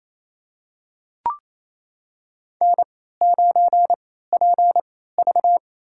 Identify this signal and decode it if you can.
Morse code, heard as a keyed tone, about 28 wpm, spaced as for 16 wpm, 700 Hz. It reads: N9PV
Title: USA Call Signs - 4 Characters - 3x Character Spacing - 28wpm